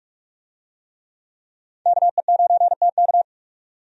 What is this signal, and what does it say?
Morse code, heard as a keyed tone, about 45 wpm, 700 Hz. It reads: KE9TK
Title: USA Call Signs - 5 Characters - 45wpm